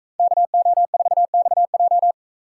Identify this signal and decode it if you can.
Morse code, heard as a keyed tone, about 36 wpm, 700 Hz. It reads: KO4XJ